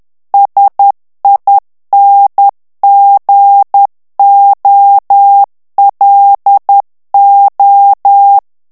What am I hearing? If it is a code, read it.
Morse code, heard as a keyed tone, about 10 wpm, 800 Hz. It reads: SINGOLO
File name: ITALIEN texte clair 2 vitesse 10 mots minute